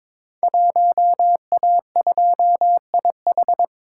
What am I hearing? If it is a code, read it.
Morse code, heard as a keyed tone, about 22 wpm, 700 Hz. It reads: 1A2IH
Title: Code Groups - 5 Letter:Numbers - 22wpm